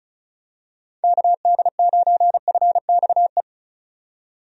Morse code, heard as a keyed tone, about 34 wpm, 700 Hz. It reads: KD9FXE
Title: USA Call Signs - 6 Characters - 35wpm